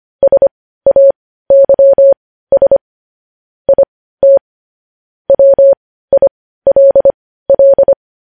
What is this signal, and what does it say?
Morse code, heard as a keyed tone, about 25 wpm, spaced as for 15 wpm, 550 Hz. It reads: SAYS IT WILL
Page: news headlines in morse code at 15 wpm